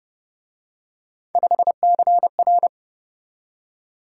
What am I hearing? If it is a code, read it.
Morse code, heard as a keyed tone, about 30 wpm, 700 Hz. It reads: 5CR